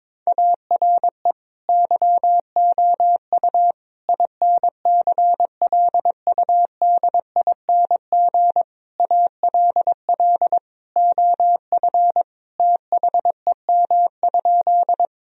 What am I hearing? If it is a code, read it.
Morse code, heard as a keyed tone, about 22 wpm, 700 Hz. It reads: ARE YOU INCLUDING ALL OF THEM?
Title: Sentences from Top 1000 Words - Review Part 4 - 22wpm